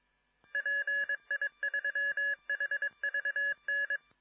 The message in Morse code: PI3HVN